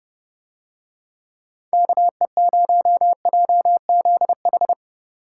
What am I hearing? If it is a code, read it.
Morse code, heard as a keyed tone, about 30 wpm, 700 Hz. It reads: KE0JZH